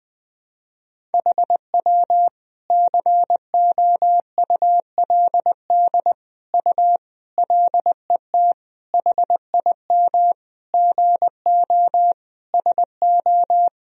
Morse code, heard as a keyed tone, about 20 wpm, 700 Hz. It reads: HW COULD U LET HIM GO SO
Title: Sentences from Top 300 Words and Abbreviations 20wpm